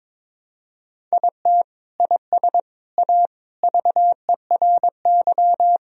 Morse code, heard as a keyed tone, about 22 wpm, 700 Hz. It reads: IT IS A VERY
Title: Sentences from Top 1000 Words - Review Part 1 - 22wpm